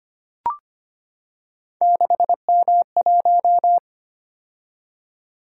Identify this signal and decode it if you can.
Morse code, heard as a keyed tone, about 25 wpm, 700 Hz. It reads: 6M1
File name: Three Letter-Number Combinations - 25wpm